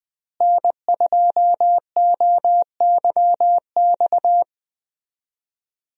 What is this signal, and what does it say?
Morse code, heard as a keyed tone, about 20 wpm, 700 Hz. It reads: N2OYX
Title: USA Call Signs - 5 Characters - Repeated 2x - 20wpm